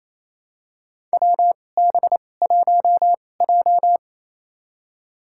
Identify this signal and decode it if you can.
Morse code, heard as a keyed tone, about 28 wpm, spaced as for 20 wpm, 700 Hz. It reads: WB1J
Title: USA Call Signs - 4 Characters - 2x Character Spacing - 28wpm